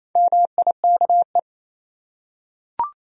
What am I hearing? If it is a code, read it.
Morse code, heard as a keyed tone, about 28 wpm, 700 Hz. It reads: MIKE
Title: USA Common Names - Max 5 Letters - 28wpm